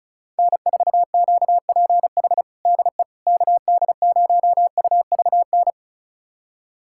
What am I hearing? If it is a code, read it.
Morse code, heard as a keyed tone, about 36 wpm, 700 Hz. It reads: N4QPH DE KD0UVN
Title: USA Callsign DE Callsign 35wpm